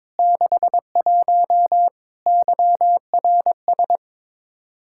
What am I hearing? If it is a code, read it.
Morse code, heard as a keyed tone, about 21 wpm, 700 Hz. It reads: 61 YRS
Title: Practice QSO - Exchange 3 - More Info - 22wpm